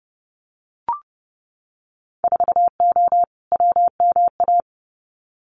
Morse code, heard as a keyed tone, about 30 wpm, 700 Hz. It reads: 4O WMA
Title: Winter Field Day Exchanges Encore 30wpm